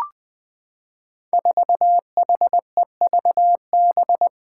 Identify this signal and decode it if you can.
Morse code, heard as a keyed tone, about 20 wpm, 700 Hz. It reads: 4HEVB
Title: Five Challenging Letter-Number Combinations 20wpm